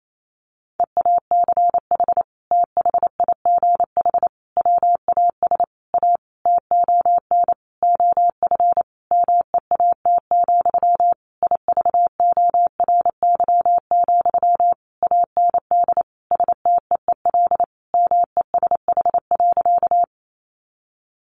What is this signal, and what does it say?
Morse code, heard as a keyed tone, about 28 wpm, 700 Hz. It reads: EACH THIGH WAS A TON OF MEAT, IVORY, AND STEEL MESH.
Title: A Sound Of Thunder 28wpm